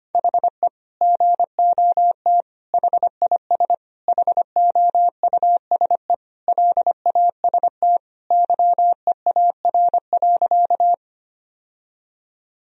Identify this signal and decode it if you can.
Morse code, heard as a keyed tone, about 25 wpm, 700 Hz. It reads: HE GOT HIS HOUSE LAST YEAR.
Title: Sentences from Top 1000 Words - Review Part 1 - 25wpm